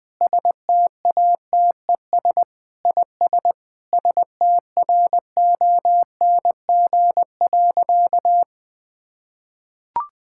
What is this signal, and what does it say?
Morse code, heard as a keyed tone, about 20 wpm, 700 Hz. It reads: STATES IS STRONG.
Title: Sentences with 8-Word Phrases - 20wpm